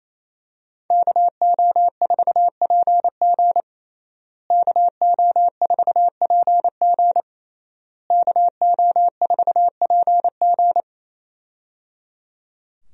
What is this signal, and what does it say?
Morse code, heard as a keyed tone, about 28 wpm, 700 Hz. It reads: KO4PG KO4PG KO4PG
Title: USA Call Signs - 5 Characters - Repeated 3x - 28wpm